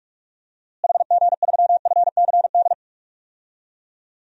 Morse code, heard as a keyed tone, about 46 wpm, 700 Hz. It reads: HG3FCD
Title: DXCC-100 Call Signs - 6 Characters - Repeated 2x - 45wpm